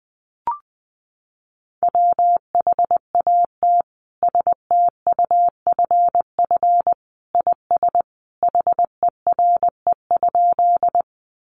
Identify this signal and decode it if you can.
Morse code, heard as a keyed tone, about 20 wpm, 700 Hz. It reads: WHAT STUFF IS HERE?
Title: Sentences from Top 1400 Words 20wpm